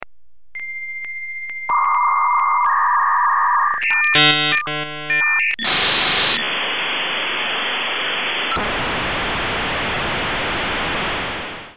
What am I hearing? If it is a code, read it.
Dial-up modem handshake.